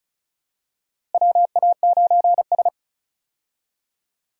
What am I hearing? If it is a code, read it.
Morse code, heard as a keyed tone, about 34 wpm, 700 Hz. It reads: WA9S
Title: USA Call Signs - 4 Characters - Repeated 3x - 35wpm